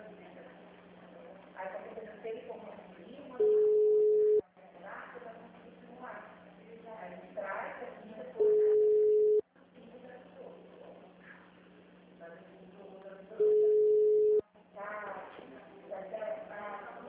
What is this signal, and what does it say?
Ringback tone (Europe, 425 Hz), with speech.
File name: out- -2005- - - .47